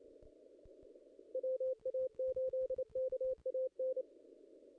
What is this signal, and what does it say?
Morse code, heard as a keyed tone, about 28 wpm, 520 Hz. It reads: WA8KAN
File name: contests / ss-cw-k5tr-op-n5rz -2300